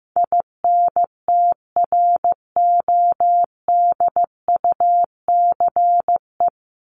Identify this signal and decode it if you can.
Morse code, heard as a keyed tone, about 15 wpm, 700 Hz. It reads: INTRODUCE
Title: ED Suffix - Top 100 Words with Base Words 15wpm